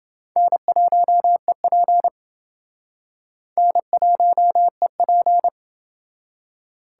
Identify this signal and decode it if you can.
Morse code, heard as a keyed tone, about 29 wpm, 700 Hz. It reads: N1EP N1EP